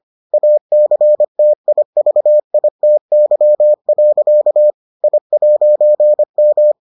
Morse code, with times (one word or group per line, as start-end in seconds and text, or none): 0.3-4.7 ACTIVITY.
5.0-6.7 I'M